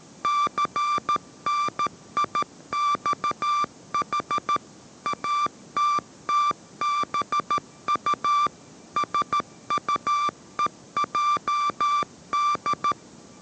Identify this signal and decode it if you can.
Morse code, heard as a keyed tone, about 13 wpm, 1180 Hz. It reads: CNIXH ATTBU SUEJD
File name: 14wpm